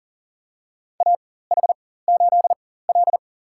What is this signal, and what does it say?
Morse code, heard as a keyed tone, about 40 wpm, spaced as for 19 wpm, 700 Hz. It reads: AH8L
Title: USA Call Signs - 4 Characters - 4x Character Spacing - 40wpm